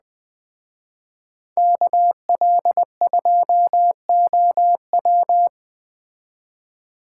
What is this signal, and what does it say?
Morse code, heard as a keyed tone, about 20 wpm, 700 Hz. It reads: KL2OW
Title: Speed Racing - USA Call Signs 17wpm